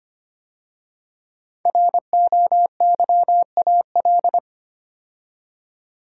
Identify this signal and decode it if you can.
Morse code, heard as a keyed tone, about 25 wpm, 700 Hz. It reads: ROYAL
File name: 5 Letter Words - First 3 Letters Given - 25wpm